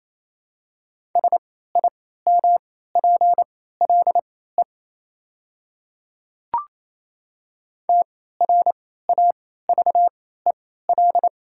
Morse code, heard as a keyed tone, about 28 wpm, spaced as for 16 wpm, 700 Hz. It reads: SIMPLE TRAVEL
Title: 3x Character Spacing - Top 500 Words - 28wpm